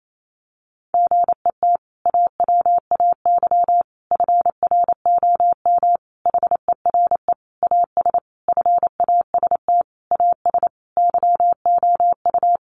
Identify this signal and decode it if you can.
Morse code, heard as a keyed tone, about 28 wpm, 700 Hz. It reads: GET AWAY FROM HERE AS FAST AS YOU
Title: Sentences from Top 600 Words 28wpm